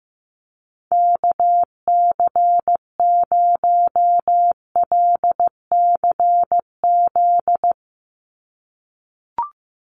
Morse code, heard as a keyed tone, about 15 wpm, 700 Hz. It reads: KC0LCZ
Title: USA Call Signs 15wpm